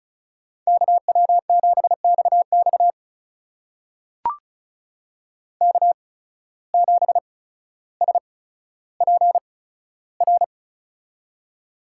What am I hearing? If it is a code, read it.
Morse code, heard as a keyed tone, about 33 wpm, spaced as for 15 wpm, 700 Hz. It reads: KW7XX K7SPR
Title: USA Call Signs - 5 Characters - 8x Character Spacing - 35wpm